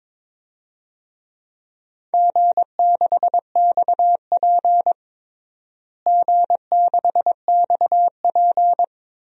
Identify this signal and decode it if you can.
Morse code, heard as a keyed tone, about 22 wpm, 700 Hz. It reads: G6XP G6XP
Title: DXCC-100 Call Signs - 4 Characters - Repeated 3x - 22wpm